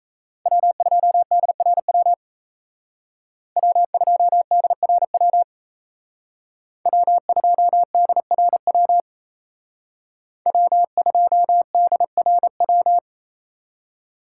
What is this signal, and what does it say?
Morse code, heard as a keyed tone, about 36 wpm, 700 Hz. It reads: W2DRW W2DRW W2DRW W2DRW